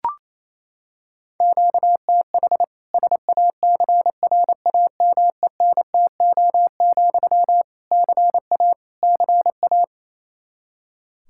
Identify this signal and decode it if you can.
Morse code, heard as a keyed tone, about 28 wpm, 700 Hz. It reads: QTH SACRAMENTO, CA CA
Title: Practice QSO - Exchange 1 - Beginning - 28wpm